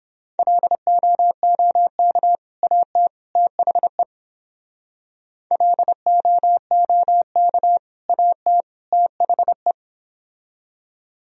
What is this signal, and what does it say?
Morse code, heard as a keyed tone, about 31 wpm, 700 Hz. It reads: LOOK AT THE LOOK AT THE